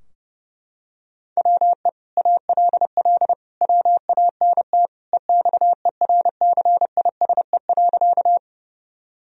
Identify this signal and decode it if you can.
Morse code, heard as a keyed tone, about 32 wpm, 700 Hz. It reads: WE ALL WANT EXERCISE.
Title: Sentences from Top 1000 Words - Review Part 4 - 30wpm